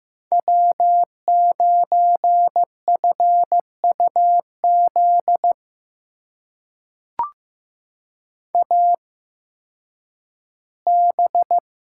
Morse code, heard as a keyed tone, about 15 wpm, spaced as for 10 wpm, 700 Hz. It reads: W9FUZ AB
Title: USA Call Signs - 5 Characters - 8x Character Spacing - 15wpm